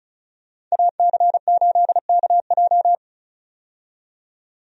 Morse code, heard as a keyed tone, about 36 wpm, 700 Hz. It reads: AC8KJ
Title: USA Call Signs - 5 Characters - 35wpm